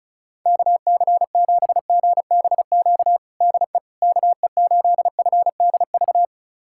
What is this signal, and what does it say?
Morse code, heard as a keyed tone, about 34 wpm, 700 Hz. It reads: KC7GBQ DE KE8FDV
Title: USA Callsign DE Callsign - 6 Characters - 35wpm